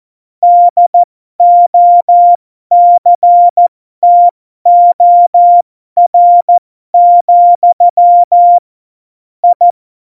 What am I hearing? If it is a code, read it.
Morse code, heard as a keyed tone, about 14 wpm, spaced as for 12 wpm, 700 Hz. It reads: DOCTOR, I
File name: Morsekurs / Witze